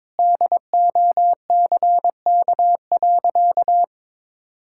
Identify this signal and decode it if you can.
Morse code, heard as a keyed tone, about 22 wpm, 700 Hz. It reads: DOCK.
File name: practice / 22wpm / Sentences with 2-Word Phrases - 22wpm